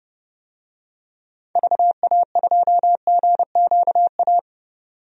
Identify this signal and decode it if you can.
Morse code, heard as a keyed tone, about 31 wpm, 700 Hz. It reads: VA2GQA
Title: DXCC-100 Call Signs - 6 Characters - 30wpm